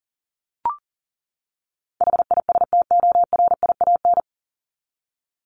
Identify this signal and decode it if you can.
Morse code, heard as a keyed tone, about 40 wpm, 700 Hz. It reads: HISTORIAN
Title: Top 1751-2000 Words 40wpm